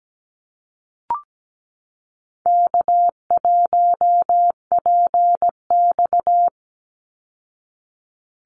Morse code, heard as a keyed tone, about 17 wpm, 700 Hz. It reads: K1PX
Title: Easy Call Signs Repeated 3x - 17wpm